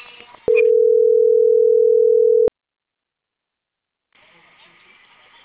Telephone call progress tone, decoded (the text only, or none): ringback tone